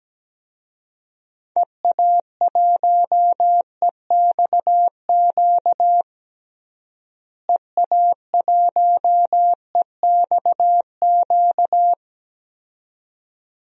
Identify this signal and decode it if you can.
Morse code, heard as a keyed tone, about 17 wpm, 700 Hz. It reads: EA1EXQ EA1EXQ
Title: DXCC-100 Call Signs - 6 Characters - Repeated 2x - 17wpm